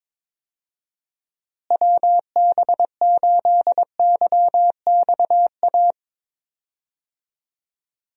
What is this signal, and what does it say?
Morse code, heard as a keyed tone, about 21 wpm, 700 Hz. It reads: WB8YXA
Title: USA Call Signs - 6 Characters - Repeated 2x - 22wpm